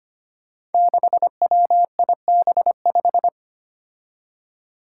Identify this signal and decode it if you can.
Morse code, heard as a keyed tone, about 25 wpm, 700 Hz. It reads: 6WIB5
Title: Five Challenging Letter-Number Combinations 25wpm